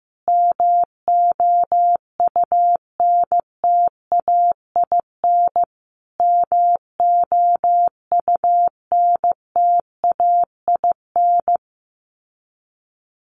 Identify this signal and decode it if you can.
Morse code, heard as a keyed tone, about 15 wpm, 700 Hz. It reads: MOUNTAIN MOUNTAIN
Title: Top 300 Words Repeated 15wpm